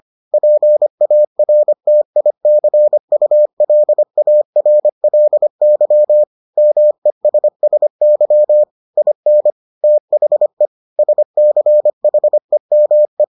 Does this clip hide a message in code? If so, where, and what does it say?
Yes: Morse code from 0.3 to 13.2 s, heard as a keyed tone, about 25 wpm, 600 Hz. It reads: PARTICULARLY MESSY IN THE SCHEME